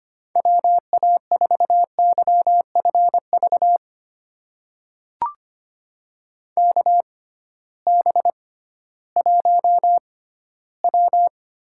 Morse code, heard as a keyed tone, about 25 wpm, spaced as for 15 wpm, 700 Hz. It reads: WA4YFV KB1W
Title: USA Call Signs - 6 Characters - 6x Character Spacing - 25wpm